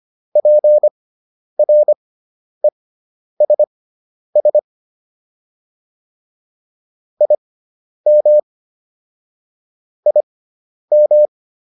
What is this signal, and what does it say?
Morse code, heard as a keyed tone, about 25 wpm, spaced as for 10 wpm, 600 Hz. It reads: PRESS IM IM